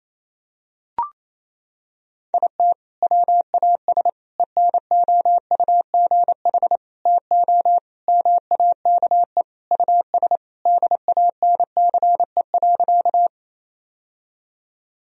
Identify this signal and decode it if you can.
Morse code, heard as a keyed tone, about 28 wpm, 700 Hz. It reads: IT WAS ENOUGH TO MAKE US DANCE.
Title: Sentences from Top 1000 Words - Review Part 4 - 28wpm